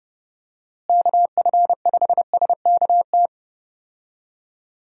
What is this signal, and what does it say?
Morse code, heard as a keyed tone, about 30 wpm, 700 Hz. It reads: KF5SKT
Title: USA Call Signs - 6 Characters - Repeated 2x - 30wpm